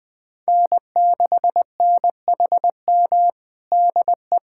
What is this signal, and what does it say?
Morse code, heard as a keyed tone, about 20 wpm, 700 Hz. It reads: N6NHM DE
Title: USA Callsign DE Callsign 20wpm